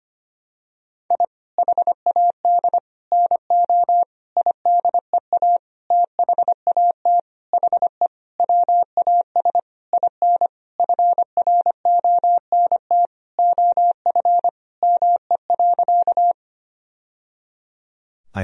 Morse code, heard as a keyed tone, about 25 wpm, 700 Hz. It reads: I HAD NO IDEA THAT HE WAS IN FRONT OF ME.
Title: Sentences from Top 1000 Words - Review Part 2 - 25wpm